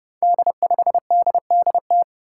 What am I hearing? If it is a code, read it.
Morse code, heard as a keyed tone, about 32 wpm, 700 Hz. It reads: D5DDT
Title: Five Challenging Letter-Number Combinations 30wpm